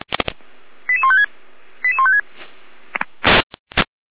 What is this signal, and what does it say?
Five-tone selective call (ZVEI), heard as quick runs of five tones, tones of about 80 ms. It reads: 88156 88156